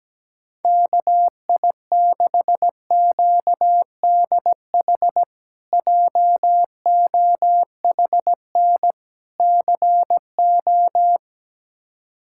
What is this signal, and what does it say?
Morse code, heard as a keyed tone, about 17 wpm, 700 Hz. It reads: KI6QDH JOHN CO
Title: K1USN SST QSO Practice - 17wpm